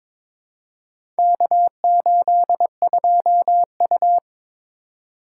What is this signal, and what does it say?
Morse code, heard as a keyed tone, about 22 wpm, 700 Hz. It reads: K82U
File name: practice / 22wpm / Four Letter-Number Combinations - 22wpm